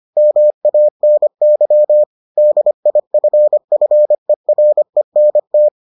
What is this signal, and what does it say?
Morse code, heard as a keyed tone, about 25 wpm, 600 Hz. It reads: MANY DIFFERENT